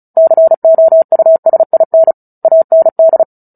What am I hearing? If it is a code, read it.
Morse code, heard as a keyed tone, about 34 wpm, 650 Hz. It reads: COUSIN AND
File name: Book Two / 09-Chapter 9 - 35 WPM